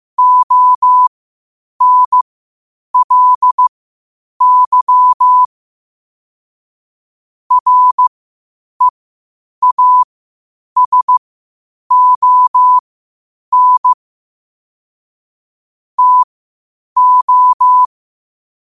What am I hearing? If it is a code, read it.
Morse code, heard as a keyed tone, about 15 wpm, spaced as for 8 wpm, 1000 Hz. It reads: ONLY REASON TO
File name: qotdfiles / 8wpm